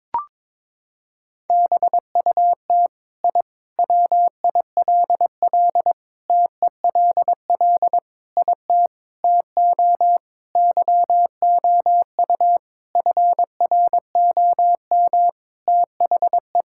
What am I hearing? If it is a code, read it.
Morse code, heard as a keyed tone, about 22 wpm, 700 Hz. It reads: BUT I WILL TELL IT TO YOU FROM THE
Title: Sentences from Top 1000 Words - Review Part 5 - 22wpm